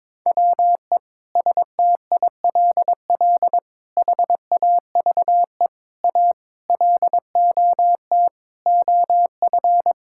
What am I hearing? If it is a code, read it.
Morse code, heard as a keyed tone, about 22 wpm, 700 Hz. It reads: WE STILL HAVE A LOT OF
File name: Sets of 9 Words - 22wpm